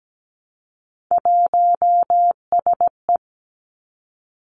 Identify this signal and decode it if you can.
Morse code, heard as a keyed tone, about 17 wpm, 700 Hz. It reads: 1SE
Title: Three Challenging Letter-Number Combinations 17wpm